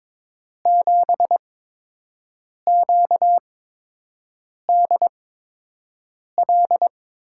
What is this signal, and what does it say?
Morse code, heard as a keyed tone, about 22 wpm, spaced as for 6 wpm, 700 Hz. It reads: 7QDL
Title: USA Call Signs - 6 Characters - 8x Character Spacing - 22wpm